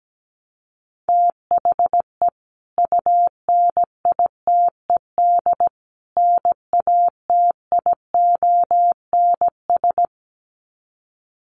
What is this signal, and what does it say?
Morse code, heard as a keyed tone, about 17 wpm, 700 Hz. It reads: THE UNITED NATIONS
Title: Sets of 3 Words 17wpm